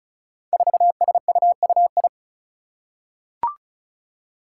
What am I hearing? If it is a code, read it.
Morse code, heard as a keyed tone, about 36 wpm, 700 Hz. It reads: 4SUUI
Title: Five Challenging Letter-Number Combinations 35wpm